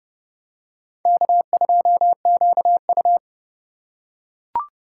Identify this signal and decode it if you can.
Morse code, heard as a keyed tone, about 28 wpm, 700 Hz. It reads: K2QU